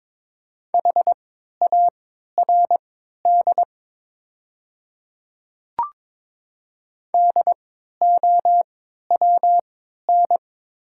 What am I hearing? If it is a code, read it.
Morse code, heard as a keyed tone, about 22 wpm, spaced as for 12 wpm, 700 Hz. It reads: HARD DOWN
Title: 3x Character Spacing - Top 300 Words - 22wpm